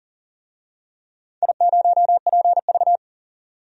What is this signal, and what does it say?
Morse code, heard as a keyed tone, about 40 wpm, 700 Hz. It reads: I0PV
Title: DXCC-100 Call Signs - 4 Characters - 40wpm